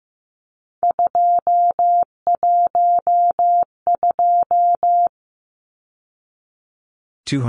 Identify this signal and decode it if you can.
Morse code, heard as a keyed tone, about 15 wpm, 700 Hz. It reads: 212
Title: Numbers - 3-Digit 15wpm